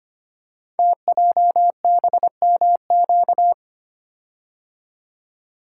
Morse code, heard as a keyed tone, about 25 wpm, 700 Hz. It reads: TJBMQ
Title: Code Groups - 5 Letters - 25wpm